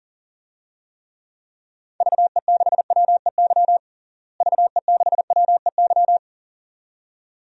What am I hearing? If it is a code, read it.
Morse code, heard as a keyed tone, about 40 wpm, 700 Hz. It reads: VE6WEY VE6WEY
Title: DXCC-100 Call Signs - 6 Characters - Repeated 2x - 40wpm